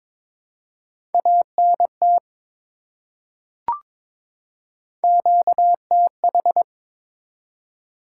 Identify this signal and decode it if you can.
Morse code, heard as a keyed tone, about 22 wpm, 700 Hz. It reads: ANT QTH
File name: practice / 22wpm / Top 20 QSO Elements - 22wpm